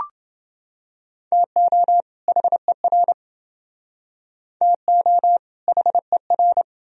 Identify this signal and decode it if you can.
Morse code, heard as a keyed tone, about 28 wpm, 700 Hz. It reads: TO HER TO HER